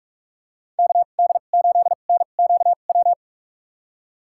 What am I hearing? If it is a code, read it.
Morse code, heard as a keyed tone, about 46 wpm, spaced as for 33 wpm, 700 Hz. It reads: KD8NQW
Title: USA Call Signs - 6 Characters - 2x Character Spacing - 45wpm